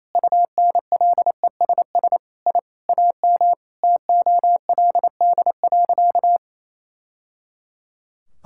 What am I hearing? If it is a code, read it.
Morse code, heard as a keyed tone, about 28 wpm, 700 Hz. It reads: UNLESS I AM TOLD.
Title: Sentences from Top 1000 Words - Review Part 6 - 28wpm